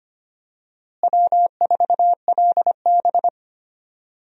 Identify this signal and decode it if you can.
Morse code, heard as a keyed tone, about 25 wpm, 700 Hz. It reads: W4LB
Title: USA Call Signs - 4 Characters - Repeated 2x - 25wpm